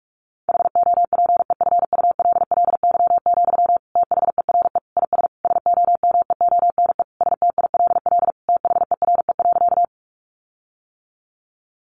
Morse code, heard as a keyed tone, about 44 wpm, 700 Hz. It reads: HOPEFULLY, THERE IS SOMEONE STILL THERE.
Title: Sentences from Top 1500 Words 45wpm